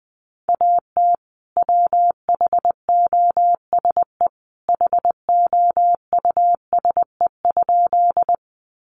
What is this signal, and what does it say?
Morse code, heard as a keyed tone, about 20 wpm, 700 Hz. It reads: AT WHOSE HOUSE?
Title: Sentences from Top 1200 Words 20wpm